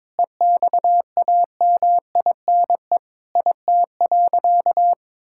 Morse code, heard as a keyed tone, about 21 wpm, 700 Hz. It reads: EXAMINE IT.
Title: Sentences from Top 1600 Words 22wpm